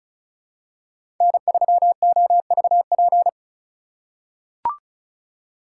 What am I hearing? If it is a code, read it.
Morse code, heard as a keyed tone, about 36 wpm, 700 Hz. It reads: N3OVP
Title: USA Call Signs - 5 Characters - 4x Character Spacing - 35wpm